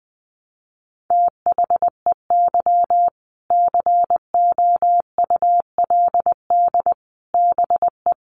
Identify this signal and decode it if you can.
Morse code, heard as a keyed tone, about 20 wpm, 700 Hz. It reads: THEY COULD BE
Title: Sentences from Top 1200 Words 20wpm